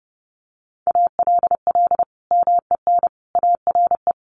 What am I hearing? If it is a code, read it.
Morse code, heard as a keyed tone, about 30 wpm, 700 Hz. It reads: ALL MEN ARE
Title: Sentences from Top 1500 Words 30wpm